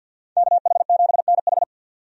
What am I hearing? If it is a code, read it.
Morse code, heard as a keyed tone, about 52 wpm, 700 Hz. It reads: KH7NH
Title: USA Call Signs - 5 Characters - 50wpm